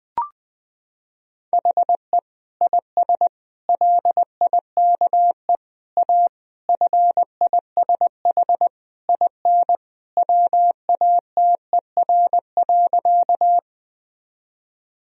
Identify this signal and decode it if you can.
Morse code, heard as a keyed tone, about 20 wpm, 700 Hz. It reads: HE IS LIKE A FISH IN WATER.
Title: Sentences from Top 1000 Words - Review Part 1 - 20wpm